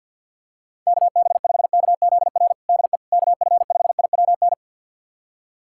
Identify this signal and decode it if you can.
Morse code, heard as a keyed tone, about 52 wpm, 700 Hz. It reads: KB5KZR DE KF5IWN
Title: USA Callsign DE Callsign - 6 Characters - 50wpm